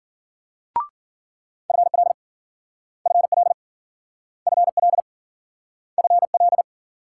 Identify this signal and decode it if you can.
Morse code, heard as a keyed tone, about 52 wpm, 700 Hz. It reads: FL FL FL FL